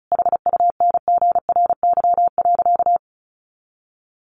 Morse code, heard as a keyed tone, about 34 wpm, 700 Hz. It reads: HUNGRY.
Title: Sentences from Top 1400 Words 35wpm